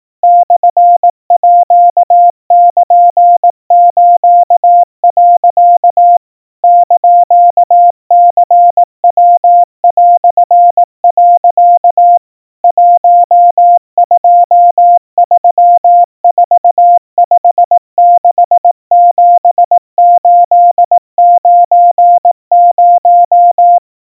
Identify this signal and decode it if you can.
Morse code (Japanese Wabun code), heard as a keyed tone, about 18 wpm, 700 Hz. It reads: モールス、 <KN>ニヤ)、 1234567890
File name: iroha 60 cpm 700 Hz Crispy